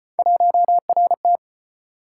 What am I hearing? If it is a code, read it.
Morse code, heard as a keyed tone, about 34 wpm, 700 Hz. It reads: 1RT